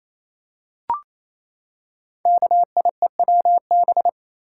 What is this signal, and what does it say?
Morse code, heard as a keyed tone, about 28 wpm, 700 Hz. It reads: KIEWB
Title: Code Groups - 5 Letters - 28wpm